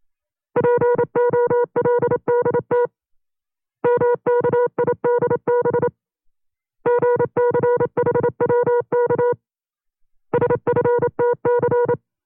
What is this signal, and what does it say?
Morse code, heard as a keyed tone, about 27 wpm, 480 Hz. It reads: POLDT MKIDB GCHWK SFTC
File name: CW-practice/25WPM